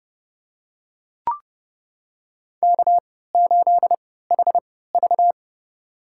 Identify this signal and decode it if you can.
Morse code, heard as a keyed tone, about 30 wpm, spaced as for 17 wpm, 700 Hz. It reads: K8HV
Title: USA Call Signs - 4 Characters - 3x Character Spacing - 30wpm